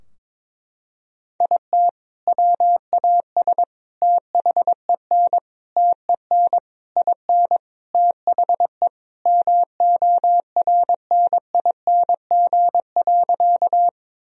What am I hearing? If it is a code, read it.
Morse code, heard as a keyed tone, about 22 wpm, 700 Hz. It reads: IT WAS THEN TEN IN THE MORNING.
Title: Sentences from Top 1000 Words - Review Part 2 - 22wpm